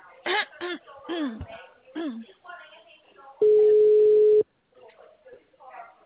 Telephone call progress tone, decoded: ringback tone

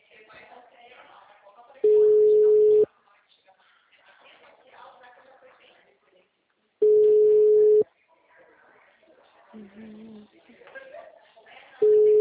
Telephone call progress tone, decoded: ringback tone

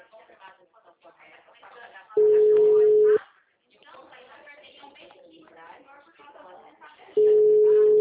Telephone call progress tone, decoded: ringback tone

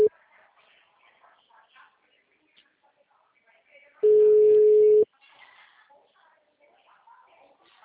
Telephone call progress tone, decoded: ringback tone

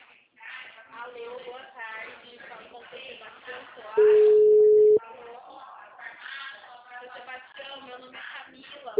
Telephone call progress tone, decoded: ringback tone

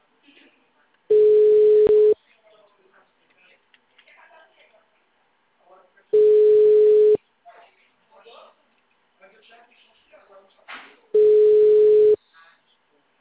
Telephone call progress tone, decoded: ringback tone

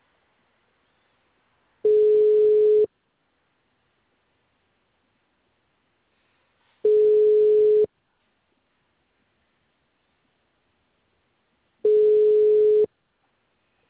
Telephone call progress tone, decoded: ringback tone

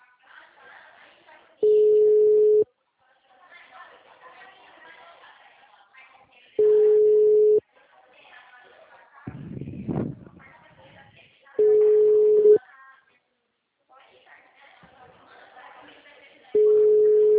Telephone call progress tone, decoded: ringback tone

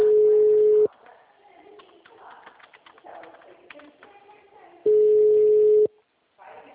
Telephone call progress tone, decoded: ringback tone

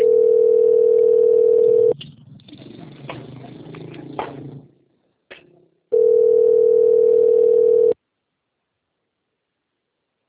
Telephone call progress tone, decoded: ringback tone

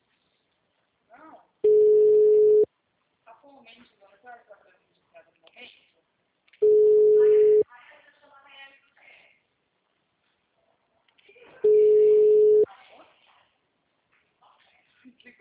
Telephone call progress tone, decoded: ringback tone

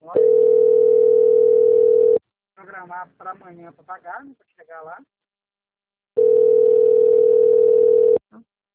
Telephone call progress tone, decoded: ringback tone

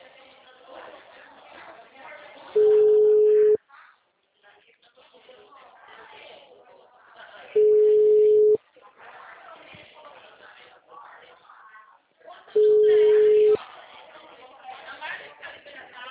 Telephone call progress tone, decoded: ringback tone

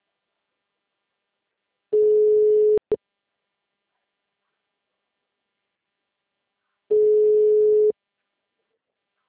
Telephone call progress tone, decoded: ringback tone

